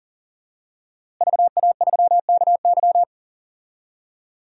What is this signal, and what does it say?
Morse code, heard as a keyed tone, about 37 wpm, 700 Hz. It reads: VA3KY
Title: DXCC-100 Call Signs - 5 Characters - Repeated 3x - 40wpm